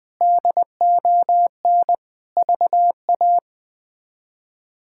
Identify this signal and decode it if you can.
Morse code, heard as a keyed tone, about 20 wpm, 700 Hz. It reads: DON VA
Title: K1USN SST QSO Practice - 20wpm